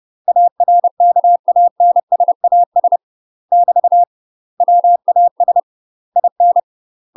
Morse code, heard as a keyed tone, about 30 wpm, 700 Hz. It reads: ARKANSAS <BT> WAS IN
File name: UK MASTER QSOs TRAINING / 30WPM / UK MASTER QSOs 0040